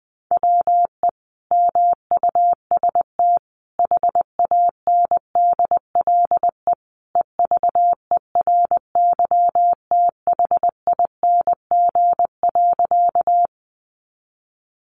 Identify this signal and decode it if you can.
Morse code, heard as a keyed tone, about 20 wpm, 700 Hz. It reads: WE MUST HANDLE EVERYTHING.
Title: Sentences from Top 1100 Words 20wpm